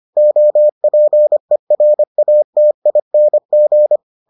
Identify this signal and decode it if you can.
Morse code, heard as a keyed tone, about 25 wpm, 600 Hz. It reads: OPERATING